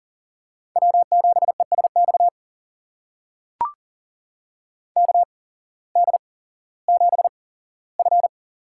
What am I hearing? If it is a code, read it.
Morse code, heard as a keyed tone, about 39 wpm, spaced as for 19 wpm, 700 Hz. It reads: W7ESX KD7F
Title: USA Call Signs - 5 Characters - 8x Character Spacing - 40wpm